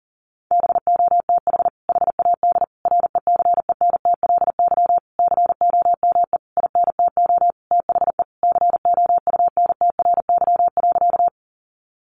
Morse code, heard as a keyed tone, about 40 wpm, 700 Hz. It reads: BOTH HAD RECENTLY COME INTO THE COUNTRY.
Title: Sentences from Top 1200 Words 40wpm